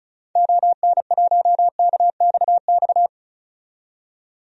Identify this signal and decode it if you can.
Morse code, heard as a keyed tone, about 36 wpm, 700 Hz. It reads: ON1KXX
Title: DXCC-100 Call Signs - 6 Characters - 35wpm